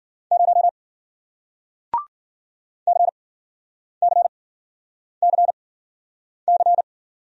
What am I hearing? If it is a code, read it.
Morse code, heard as a keyed tone, about 54 wpm, spaced as for 9 wpm, 700 Hz. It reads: 0 CCCC